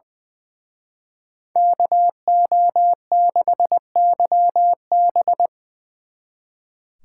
Morse code, heard as a keyed tone, about 23 wpm, 700 Hz. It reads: KO6YB
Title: Speed Racing - USA 2x2 Call Signs 20wpm